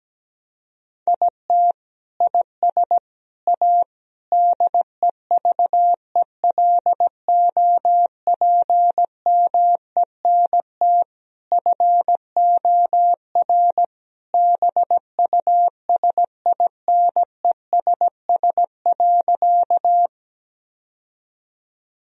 Morse code, heard as a keyed tone, about 17 wpm, 700 Hz. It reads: IT IS A DEVELOPMENT FOR BUSINESS.
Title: Sentences from Top 1000 Words - Review Part 4 - 17wpm